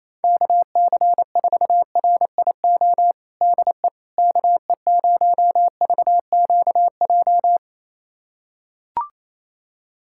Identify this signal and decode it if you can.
Morse code, heard as a keyed tone, about 28 wpm, 700 Hz. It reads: KC4RIO DE KE0VQJ
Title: USA Callsign DE Callsign 28wpm